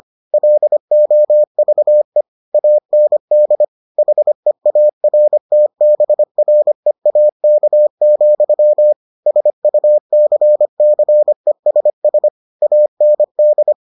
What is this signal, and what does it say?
Morse code, heard as a keyed tone, about 25 wpm, 600 Hz. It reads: LOVE AND HEARTBREAK, SUCCESS AND